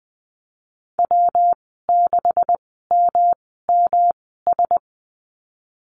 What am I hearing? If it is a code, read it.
Morse code, heard as a keyed tone, about 20 wpm, spaced as for 14 wpm, 700 Hz. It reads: W6MMS
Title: 140 Lesson 23g - Letter f - Call Signs 20wpm